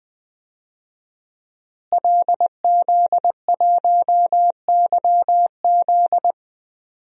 Morse code, heard as a keyed tone, about 20 wpm, 700 Hz. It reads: LZ1YZ
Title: DXCC-100 Call Signs - 5 Characters - Repeated 2x - 20wpm